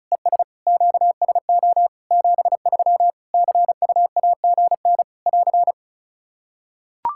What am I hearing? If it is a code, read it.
Morse code, heard as a keyed tone, about 34 wpm, 700 Hz. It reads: ES QSO 73 CUAGN <AR>
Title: K1USN SST QSO Practice 35wpm